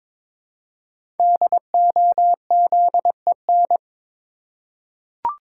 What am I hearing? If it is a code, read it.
Morse code, heard as a keyed tone, about 22 wpm, 700 Hz. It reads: DOZEN
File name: 5 Letter Words - First 3 Letters Given - 22wpm